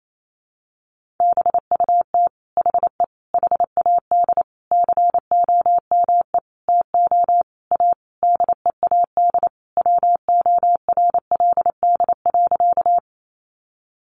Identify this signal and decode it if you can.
Morse code, heard as a keyed tone, about 28 wpm, 700 Hz. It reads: BUT HE HAD COME TO A DEAD WORLD.
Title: Sentences from Top 1500 Words 28wpm